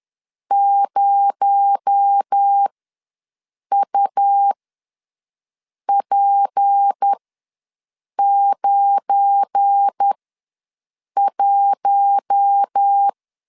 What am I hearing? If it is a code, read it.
Morse code, heard as a keyed tone, about 10 wpm, spaced as for 6 wpm, 800 Hz. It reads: 0UP91
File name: LEZIONI / 7 GF9Z / 7 GF9 ws 10wpm 4034